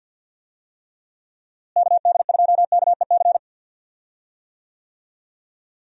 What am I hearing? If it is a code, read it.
Morse code, heard as a keyed tone, about 52 wpm, 700 Hz. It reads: KD2KEC
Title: USA Call Signs - 6 Characters - Repeated 2x - 50wpm